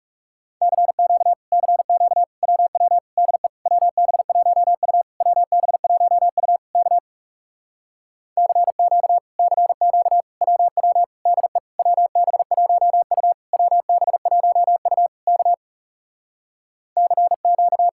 Morse code, heard as a keyed tone, about 40 wpm, 700 Hz. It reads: CQ CQ WW DE WB1U WB1U K CQ CQ WW DE WB1U WB1U K CQ